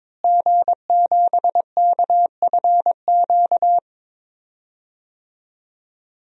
Morse code, heard as a keyed tone, about 21 wpm, 700 Hz. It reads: G7KFQ
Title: DXCC-100 Call Signs - 5 Characters - Repeated 2x - 22wpm